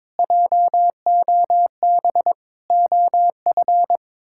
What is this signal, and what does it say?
Morse code, heard as a keyed tone, about 22 wpm, 700 Hz. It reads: JOB OF
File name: practice / 22wpm / Sets of 7 Words - 22wpm